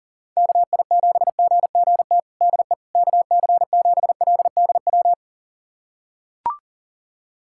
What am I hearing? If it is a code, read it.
Morse code, heard as a keyed tone, about 40 wpm, 700 Hz. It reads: KI7GGT DE KC7LDW
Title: USA Callsign DE Callsign - 6 Characters - 40wpm